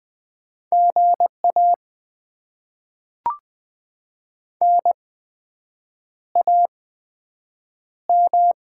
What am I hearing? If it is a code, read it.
Morse code, heard as a keyed tone, about 20 wpm, spaced as for 7 wpm, 700 Hz. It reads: GA NAM
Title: Top 20 QSO Elements - 8x Character Spacing - 20wpm